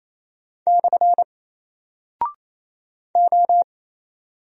Morse code, heard as a keyed tone, about 27 wpm, 700 Hz. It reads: / O
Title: Single Character (Letter:Number:Punctuation:Prosign) 28wpm